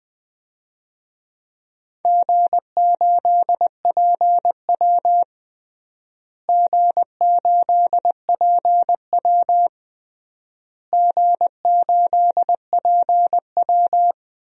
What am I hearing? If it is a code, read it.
Morse code, heard as a keyed tone, about 20 wpm, 700 Hz. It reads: G8PW G8PW G8PW
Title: DXCC-100 Call Signs - 4 Characters - Repeated 3x - 20wpm